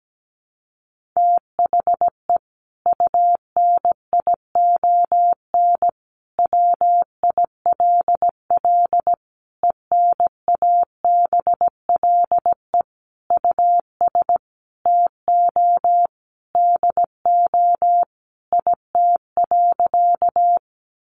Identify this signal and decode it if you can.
Morse code, heard as a keyed tone, about 17 wpm, 700 Hz. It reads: THE UNION WILL ENABLE US TO DO IT.
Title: Sentences from Top 1500 Words 17wpm